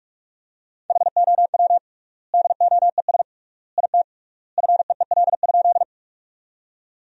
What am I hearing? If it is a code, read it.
Morse code, heard as a keyed tone, about 46 wpm, 700 Hz. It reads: HOW DOES IT FEEL?
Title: 3x Word-Spacing - Sentences from Top 400 Words - 45wpm